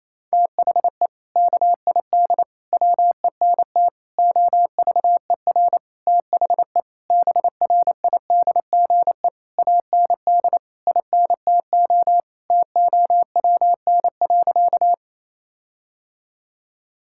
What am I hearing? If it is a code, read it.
Morse code, heard as a keyed tone, about 28 wpm, 700 Hz. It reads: THE KID WENT OVER THE BRIDGE AND INTO TOWN.
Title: Sentences from Top 1600 Words 28wpm